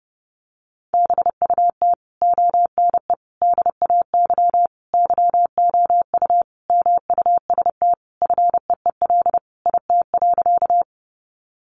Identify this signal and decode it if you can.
Morse code, heard as a keyed tone, about 28 wpm, 700 Hz. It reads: BUT ONE DAY YOU MUST FEEL IT.
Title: Sentences from Top 400 Words 30wpm